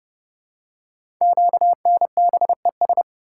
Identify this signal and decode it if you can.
Morse code, heard as a keyed tone, about 28 wpm, 700 Hz. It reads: QNBES
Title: Code Groups - 5 Letters - 30wpm